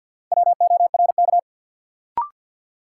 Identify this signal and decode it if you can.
Morse code, heard as a keyed tone, about 54 wpm, 700 Hz. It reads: WORK
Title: Sets of 2 Words - New Word Drill - 50wpm